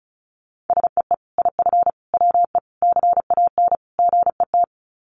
Morse code, heard as a keyed tone, about 34 wpm, 700 Hz. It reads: SEE IF WE CAN GET
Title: Sets of 5 Words Encore2 35wpm